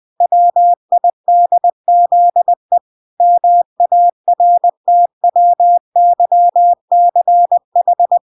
Morse code, heard as a keyed tone, about 20 wpm, 700 Hz. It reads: WIDZE MARTWYCH